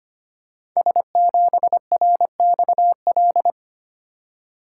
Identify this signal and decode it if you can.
Morse code, heard as a keyed tone, about 25 wpm, 700 Hz. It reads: S7RXL
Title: Code Groups - 5 Letter:Numbers - 25wpm